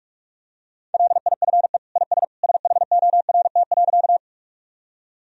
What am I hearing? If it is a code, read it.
Morse code, heard as a keyed tone, about 46 wpm, 700 Hz. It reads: LIFE IS SHORT.
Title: Sentences from Top 1000 Words - Review Part 2 - 45wpm